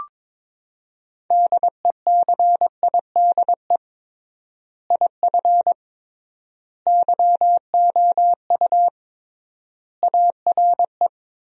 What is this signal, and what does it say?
Morse code, heard as a keyed tone, about 22 wpm, 700 Hz. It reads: DECIDE IF YOU ARE
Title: 3x Word-Spacing - Sentences from Top 500 Words - 22wpm